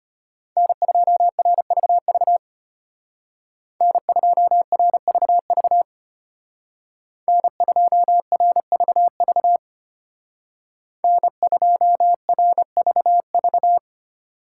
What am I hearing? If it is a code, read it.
Morse code, heard as a keyed tone, about 31 wpm, 700 Hz. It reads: N2RVV N2RVV N2RVV N2RVV